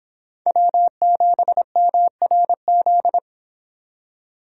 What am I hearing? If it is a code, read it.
Morse code, heard as a keyed tone, about 26 wpm, 700 Hz. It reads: W7MRZ